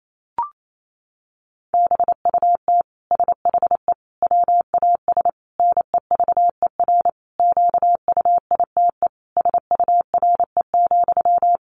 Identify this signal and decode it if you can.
Morse code, heard as a keyed tone, about 28 wpm, 700 Hz. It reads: BUT SHE WAS NEVER QUITE SURE,
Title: Sentences from Top 700 Words 28wpm